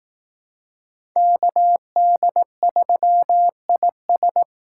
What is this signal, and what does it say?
Morse code, heard as a keyed tone, about 18 wpm, 700 Hz. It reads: KD3IS
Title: Speed Racing - USA 2x2 Call Signs 15wpm